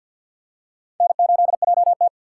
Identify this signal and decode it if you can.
Morse code, heard as a keyed tone, about 53 wpm, 700 Hz. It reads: N8JT
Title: USA Call Signs - 4 Characters - 6x Character Spacing - 50wpm